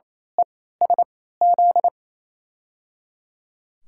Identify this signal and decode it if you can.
Morse code, heard as a keyed tone, about 28 wpm, spaced as for 16 wpm, 700 Hz. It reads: ESZ
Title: USA Call Signs - 3x Character Spacing - 28wpm